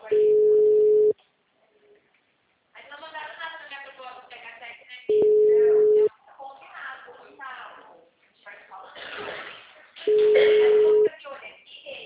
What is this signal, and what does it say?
Ringback tone (Europe, 425 Hz), with speech.